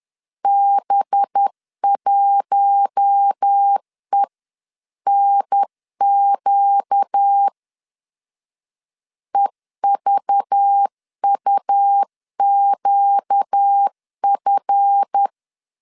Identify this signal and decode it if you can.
Morse code, heard as a keyed tone, about 10 wpm, 800 Hz. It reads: B1ENQ EVUQF
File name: LEZIONI / 7 GF9Z / 7 GF 10wpm 2514